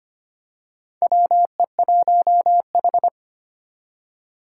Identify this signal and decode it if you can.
Morse code, heard as a keyed tone, about 25 wpm, 700 Hz. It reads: WE1H
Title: Four Challenging Letter-Number Combinations 25wpm